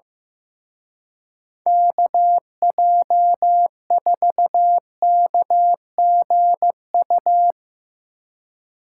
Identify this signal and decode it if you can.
Morse code, heard as a keyed tone, about 15 wpm, 700 Hz. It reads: KJ4KGU
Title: Speed Racing - USA Call Signs 15wpm